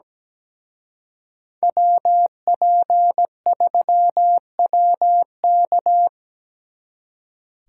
Morse code, heard as a keyed tone, about 20 wpm, 700 Hz. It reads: WP3WK
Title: Speed Racing - USA 2x2 Call Signs 17wpm